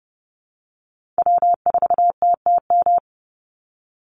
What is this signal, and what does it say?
Morse code, heard as a keyed tone, about 28 wpm, 700 Hz. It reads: W4TTM
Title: Call Signs Repeated 3x - 30wpm